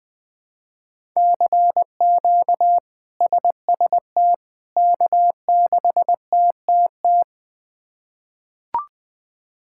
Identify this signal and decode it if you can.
Morse code, heard as a keyed tone, about 20 wpm, 700 Hz. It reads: CQ SST K6TTT
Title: K1USN SST QSO Practice - 20wpm